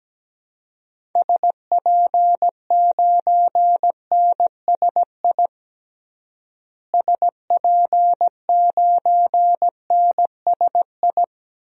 Morse code, heard as a keyed tone, about 17 wpm, 700 Hz. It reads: SP9NSI SP9NSI
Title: DXCC-100 Call Signs - 6 Characters - Repeated 2x - 17wpm